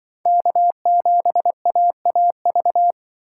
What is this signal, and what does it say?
Morse code, heard as a keyed tone, about 24 wpm, 700 Hz. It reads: K7AAV
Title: Speed Racing - USA 1x3 Call Signs 20wpm